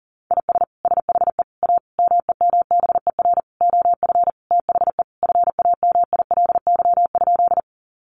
Morse code, heard as a keyed tone, about 40 wpm, 700 Hz. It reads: IS SHE A MEMBER OF THE FAMILY?
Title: Sentences from Top 1100 Words 40wpm